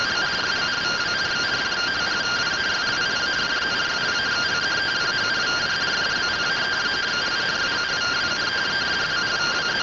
Two-tone teleprinter signal, about 170 Hz shift.